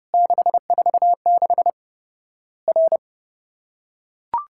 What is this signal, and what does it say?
Morse code, heard as a keyed tone, about 30 wpm, 700 Hz. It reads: 646 R
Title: Worked All Europe DX Contest - QTC Exchange - 30wpm